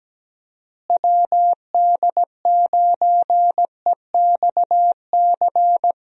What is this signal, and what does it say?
Morse code, heard as a keyed tone, about 17 wpm, 700 Hz. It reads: WD9EXC
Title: USA Call Signs - 6 Characters - Repeated 3x - 17wpm